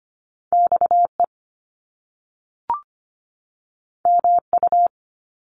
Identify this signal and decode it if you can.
Morse code, heard as a keyed tone, about 25 wpm, 700 Hz. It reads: XE MU
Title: Top Two Letter Combinations in Words 25wpm